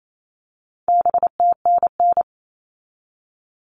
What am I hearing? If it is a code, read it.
Morse code, heard as a keyed tone, about 28 wpm, 700 Hz. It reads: BTNN
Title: All Cut Numbers - 4-Digit 28wpm